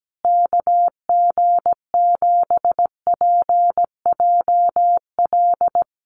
Morse code, heard as a keyed tone, about 17 wpm, 700 Hz. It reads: KG7PJL
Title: USA Call Signs Repeated 3x 17wpm